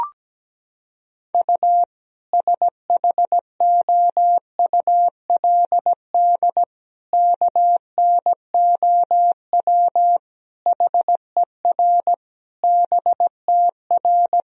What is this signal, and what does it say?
Morse code, heard as a keyed tone, about 17 wpm, 700 Hz. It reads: U SHOULD KNOW HER BTR
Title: Sentences from Top 400 Words and Abbreviations 17wpm